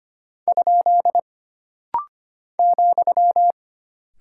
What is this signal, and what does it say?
Morse code, heard as a keyed tone, about 25 wpm, 700 Hz. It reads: ? ,
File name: Single Punctuation-Prosign - Rapid-Fire - 25wpm